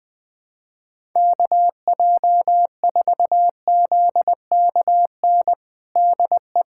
Morse code, heard as a keyed tone, about 20 wpm, 700 Hz. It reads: KJ4ZKN DE
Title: USA Callsign DE Callsign - 6 Characters - 20wpm